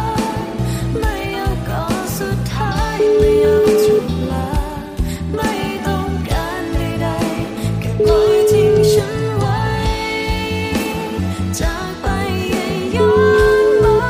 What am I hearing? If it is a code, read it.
Ringback tone (Europe, 425 Hz), with noise.